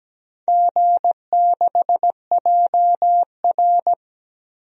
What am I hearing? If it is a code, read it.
Morse code, heard as a keyed tone, about 17 wpm, 700 Hz. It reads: G6JR
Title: DXCC-100 Call Signs - 4 Characters - 17wpm